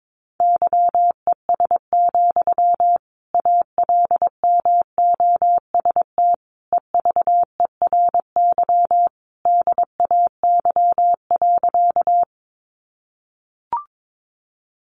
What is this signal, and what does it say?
Morse code, heard as a keyed tone, about 22 wpm, 700 Hz. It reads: YES, ALMOST EVERY DAY.
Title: Sentences from Top 700 Words 22wpm